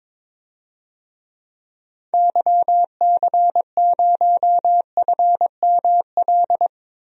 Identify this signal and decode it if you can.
Morse code, heard as a keyed tone, about 22 wpm, 700 Hz. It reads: YC0FML
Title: DXCC-100 Call Signs - 6 Characters - 22wpm